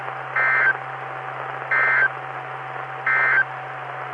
Emergency Alert System (SAME) tones, heard as harsh buzzing bursts. It is heard from 0.3 to 3.5 s.
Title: Saline, AK SKYWARN Repeater System - W5RHS